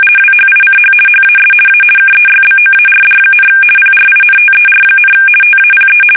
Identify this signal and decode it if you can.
RTTY, about 50 baud. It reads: WHICH HAS A KEYBOARD, PRINTER, TAPE
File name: fsk850-50B / teletype-corp